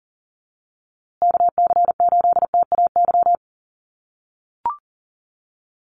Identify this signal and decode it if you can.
Morse code, heard as a keyed tone, about 40 wpm, 700 Hz. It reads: KC8TAY
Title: USA Call Signs Repeated 2x 40wpm